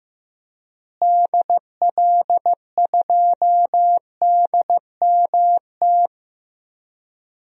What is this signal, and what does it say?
Morse code, heard as a keyed tone, about 15 wpm, 700 Hz. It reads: DL2DMT
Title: DXCC-100 Call Signs - 6 Characters - Repeated 2x - 15wpm